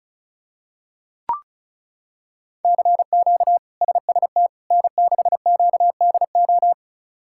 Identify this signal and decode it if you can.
Morse code, heard as a keyed tone, about 36 wpm, 700 Hz. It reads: CQ SST N6QDO
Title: K1USN SST QSO Practice - 35wpm